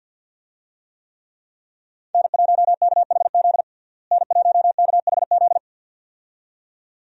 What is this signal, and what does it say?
Morse code, heard as a keyed tone, about 51 wpm, 700 Hz. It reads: N1KHZ N1KHZ
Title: USA Call Signs - 5 Characters - Repeated 2x - 50wpm